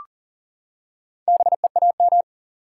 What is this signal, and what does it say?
Morse code, heard as a keyed tone, about 40 wpm, 700 Hz. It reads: BEAM
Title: Top 180 QSO Elements 40wpm